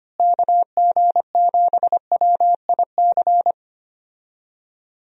Morse code, heard as a keyed tone, about 25 wpm, 700 Hz. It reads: KG7WIC
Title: USA Call Signs - 6 Characters - Repeated 2x - 25wpm